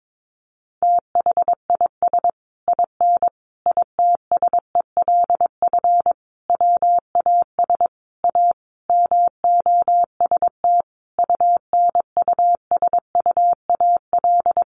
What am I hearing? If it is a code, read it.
Morse code, heard as a keyed tone, about 22 wpm, 700 Hz. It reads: THIS IN ITSELF WAS A MOST UNUSUAL
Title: Sentences from Top 1400 Words 22wpm